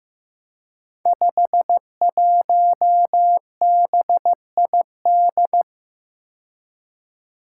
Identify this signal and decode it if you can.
Morse code, heard as a keyed tone, about 15 wpm, 700 Hz. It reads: 51BID
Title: Five Challenging Letter-Number Combinations 15wpm